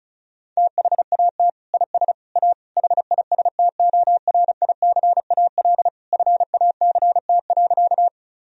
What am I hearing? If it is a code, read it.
Morse code, heard as a keyed tone, about 36 wpm, 700 Hz. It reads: THAT IS A HISTORICAL FACT.
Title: Sentences from Top 1000 Words - Review Part 4 - 35wpm